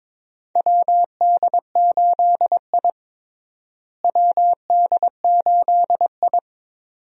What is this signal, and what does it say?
Morse code, heard as a keyed tone, about 22 wpm, 700 Hz. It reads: WD8I WD8I
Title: USA Call Signs - 4 Characters - Repeated 2x - 22wpm